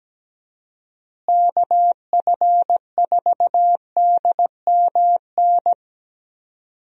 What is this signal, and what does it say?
Morse code, heard as a keyed tone, about 17 wpm, 700 Hz. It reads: KF4DMN
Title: USA Call Signs - 6 Characters - Repeated 3x - 17wpm